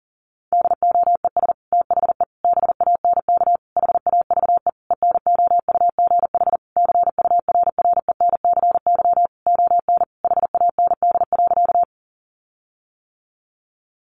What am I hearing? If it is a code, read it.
Morse code, heard as a keyed tone, about 40 wpm, 700 Hz. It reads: DOES THE BANK HAVE ENOUGH CURRENCY ON HAND.
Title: Sentences from Top 1500 Words 40wpm